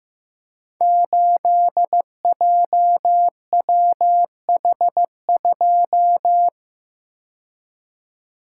Morse code, heard as a keyed tone, about 15 wpm, 700 Hz. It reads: 8JWH2
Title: Code Groups - 5 Letter:Numbers - 15wpm